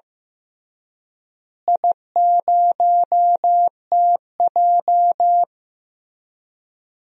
Morse code, heard as a keyed tone, about 15 wpm, 700 Hz. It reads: I0TJ
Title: DXCC-100 Call Signs - 4 Characters - Repeated 3x - 15wpm